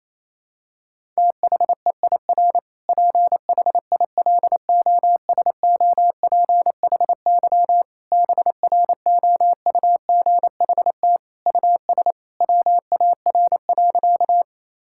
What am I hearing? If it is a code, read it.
Morse code, heard as a keyed tone, about 28 wpm, 700 Hz. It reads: THEIR PHILOSOPHY BROUGHT US WAR.
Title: Sentences from Top 1000 Words - Review Part 6 - 28wpm